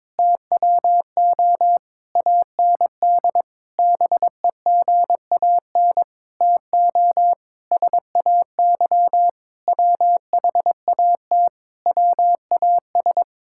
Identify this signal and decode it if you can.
Morse code, heard as a keyed tone, about 22 wpm, 700 Hz. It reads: TWO AND BEGAN TO SAY WHAT WAS
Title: Sentences from Top 1000 Words - Review Part 5 - 22wpm